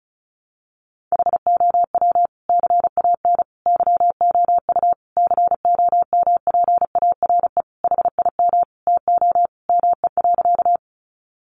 Morse code, heard as a keyed tone, about 34 wpm, 700 Hz. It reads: HOW CAN YOU COMPARE HIM TO ME.
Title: Sentences from Top 1300 Words 35wpm